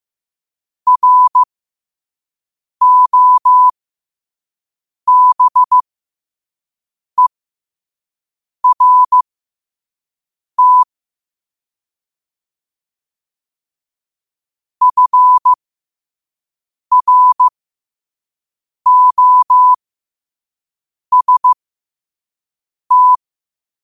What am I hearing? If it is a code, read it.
Morse code, heard as a keyed tone, about 15 wpm, spaced as for 5 wpm, 1000 Hz. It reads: ROBERT FROST
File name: qotdfiles / 5wpm